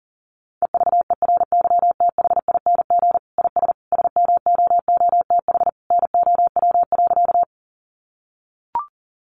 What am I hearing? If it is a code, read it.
Morse code, heard as a keyed tone, about 40 wpm, 700 Hz. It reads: EVERYTHING IS SMOOTH NOW.
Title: Sentences from Top 1400 Words 40wpm